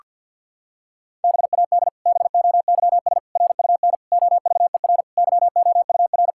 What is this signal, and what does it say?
Morse code, heard as a keyed tone, about 52 wpm, 700 Hz. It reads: BAD BOYS RUN OVER YOUR
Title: Pangram Challenge v2 - 50wpm